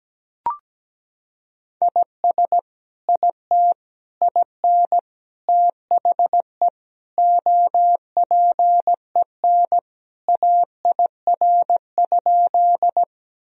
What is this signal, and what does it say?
Morse code, heard as a keyed tone, about 17 wpm, 700 Hz. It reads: IS IT IN THE OPEN AIR?
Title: Sentences from Top 1000 Words - Review Part 1 - 17wpm